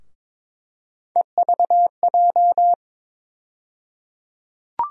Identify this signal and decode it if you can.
Morse code, heard as a keyed tone, about 22 wpm, 700 Hz. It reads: EVJ
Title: USA Call Sign Postfixes - 3 Letters 22wpm